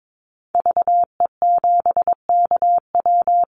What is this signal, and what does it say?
Morse code, heard as a keyed tone, about 22 wpm, 700 Hz. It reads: VE7KW
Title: Call Signs - 22wpm